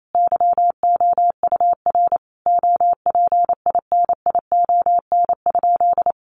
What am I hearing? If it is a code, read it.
Morse code, heard as a keyed tone, about 28 wpm, 700 Hz. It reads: YOUR OPINION?
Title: Sentences from Top 1200 Words 28wpm